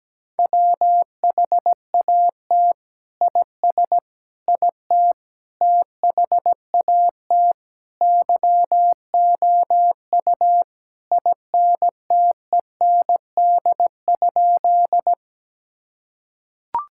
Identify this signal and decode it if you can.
Morse code, heard as a keyed tone, about 17 wpm, 700 Hz. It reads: WHAT IS IT THAT YOU INTEND?
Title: Sentences from Top 1600 Words 17wpm